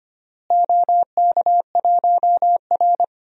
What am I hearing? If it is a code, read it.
Morse code, heard as a keyed tone, about 25 wpm, 700 Hz. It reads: OK1R
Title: DXCC-100 Call Signs - 4 Characters - Repeated 2x - 25wpm